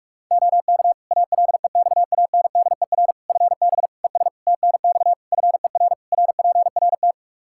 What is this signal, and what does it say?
Morse code, heard as a keyed tone, about 47 wpm, 700 Hz. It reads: OK ALEXANDER FB ES TNX FER RPRT
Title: K1USN SST QSO Practice 45wpm